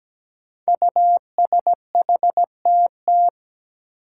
Morse code, heard as a keyed tone, about 17 wpm, 700 Hz. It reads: USHTT
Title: Five Challenging Letter-Number Combinations 17wpm